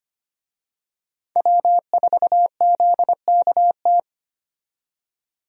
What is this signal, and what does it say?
Morse code, heard as a keyed tone, about 25 wpm, 700 Hz. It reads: W4ZKT
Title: USA Call Signs - 5 Characters - 25wpm